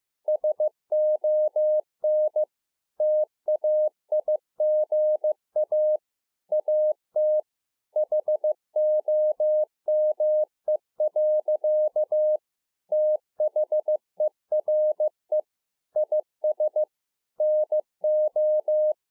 Morse code, heard as a keyed tone, about 15 wpm, 610 Hz. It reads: SON TAIGA AT HOME. THERE IS NO